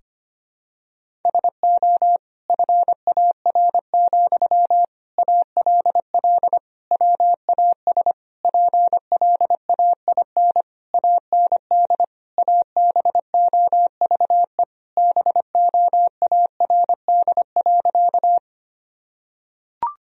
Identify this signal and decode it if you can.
Morse code, heard as a keyed tone, about 25 wpm, 700 Hz. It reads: SO FAR, ALL WAS PLAIN AND ABOVE BOARD.
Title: Sentences from Top 1000 Words - Review Part 4 - 25wpm